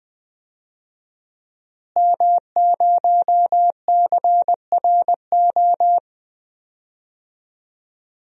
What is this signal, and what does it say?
Morse code, heard as a keyed tone, about 20 wpm, 700 Hz. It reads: M0CRO
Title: DXCC-100 Call Signs - 5 Characters - Repeated 2x - 20wpm